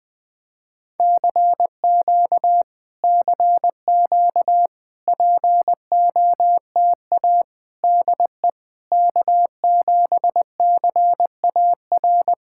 Morse code, heard as a keyed tone, about 20 wpm, 700 Hz. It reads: CQ CQ POTA DE K7CAR
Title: POTA QSO Practice - 20wpm